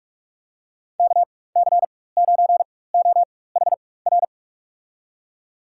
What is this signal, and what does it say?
Morse code, heard as a keyed tone, about 47 wpm, spaced as for 21 wpm, 700 Hz. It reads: KC9OHR
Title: USA Call Signs - 4x Character Spacing - 45wpm